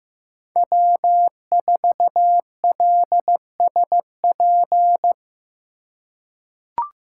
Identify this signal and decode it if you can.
Morse code, heard as a keyed tone, about 15 wpm, 700 Hz. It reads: W4LSP
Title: USA Call Signs - 5 Characters - 3x Character Spacing - 15wpm